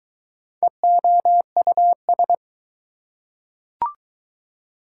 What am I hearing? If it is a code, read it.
Morse code, heard as a keyed tone, about 20 wpm, 700 Hz. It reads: EOUS